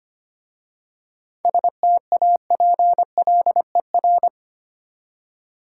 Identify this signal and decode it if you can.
Morse code, heard as a keyed tone, about 25 wpm, 700 Hz. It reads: STAPLER
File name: practice / 25wpm / Easy Words with Hints - 25wpm